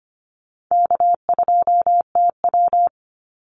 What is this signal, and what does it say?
Morse code, heard as a keyed tone, about 25 wpm, 700 Hz. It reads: K2TW
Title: Easy Call Signs Repeated 2x - 25wpm